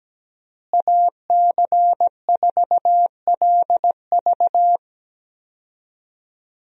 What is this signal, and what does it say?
Morse code, heard as a keyed tone, about 17 wpm, 700 Hz. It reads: AC4LV
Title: USA Call Signs - 5 Characters - Repeated 2x - 17wpm